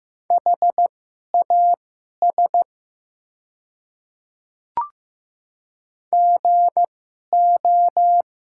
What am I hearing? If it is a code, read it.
Morse code, heard as a keyed tone, about 15 wpm, spaced as for 11 wpm, 700 Hz. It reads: HAS GO
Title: 2x Character Spacing - Top 300 Words - 15wpm